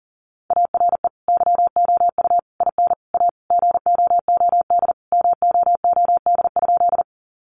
Morse code, heard as a keyed tone, about 37 wpm, 700 Hz. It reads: ARE YOU IN A GOOD MOOD?
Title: Sentences from Top 1100 Words 40wpm